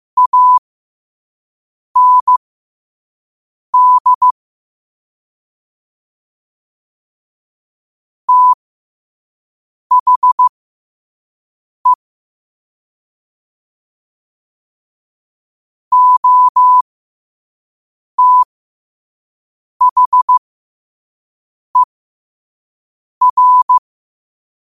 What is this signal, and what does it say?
Morse code, heard as a keyed tone, about 15 wpm, spaced as for 5 wpm, 1000 Hz. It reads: AND THE OTHER
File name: qotdfiles / 5wpm